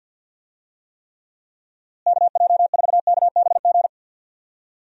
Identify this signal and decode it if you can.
Morse code, heard as a keyed tone, about 45 wpm, 700 Hz. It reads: KJ4KBG
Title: USA Call Signs - 6 Characters - Repeated 2x - 50wpm